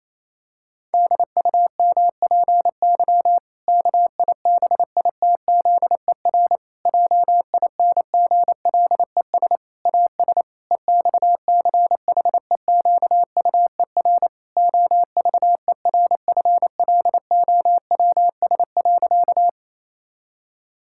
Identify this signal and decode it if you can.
Morse code, heard as a keyed tone, about 28 wpm, 700 Hz. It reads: DUMPY KIBITZER JINGLES AS EXCHEQUER OVERFLOWS.
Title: Pangram Challenge v2 - 28wpm